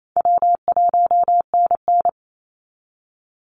Morse code, heard as a keyed tone, about 28 wpm, 700 Hz. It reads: W1NN
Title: Call Signs Repeated 3x - 28wpm